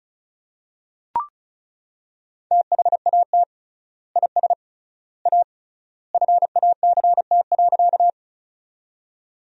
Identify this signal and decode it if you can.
Morse code, heard as a keyed tone, about 36 wpm, 700 Hz. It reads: THAT IS A FACT.
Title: 3x Word-Spacing - Sentences from Top 500 Words - 35wpm